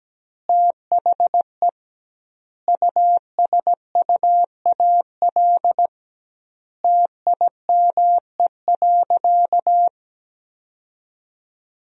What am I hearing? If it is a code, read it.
Morse code, heard as a keyed tone, about 17 wpm, 700 Hz. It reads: THE USUAL TIME.
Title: 2x Word-Spacing - Sentences from Top 400 Words - 17wpm